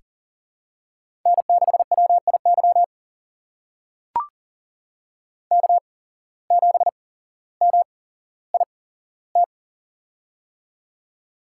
Morse code, heard as a keyed tone, about 43 wpm, spaced as for 17 wpm, 700 Hz. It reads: N6WIY K7MIT
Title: USA Call Signs - 5 Characters - 8x Character Spacing - 40wpm